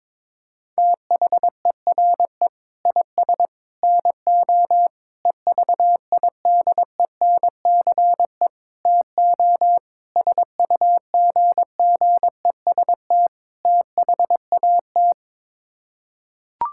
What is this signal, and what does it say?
Morse code, heard as a keyed tone, about 22 wpm, 700 Hz. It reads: THERE IS NO EVIDENCE TO SUGGEST THAT
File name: practice / 22wpm / Sets of 7 Words - 22wpm